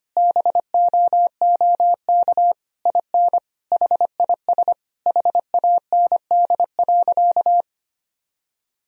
Morse code, heard as a keyed tone, about 25 wpm, 700 Hz. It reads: BOOK IN HIS HAND.
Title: Sentences from Top 1000 Words - Review Part 1 - 25wpm